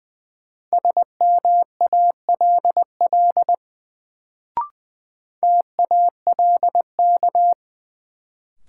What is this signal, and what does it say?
Morse code, heard as a keyed tone, about 20 wpm, 700 Hz. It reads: SMALL TALK
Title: Sets of 2 Words - New Word Drill - 20wpm